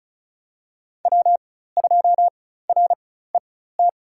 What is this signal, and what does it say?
Morse code, heard as a keyed tone, about 36 wpm, spaced as for 16 wpm, 700 Hz. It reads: W2RET
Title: USA Call Signs - 5 Characters - 4x Character Spacing - 35wpm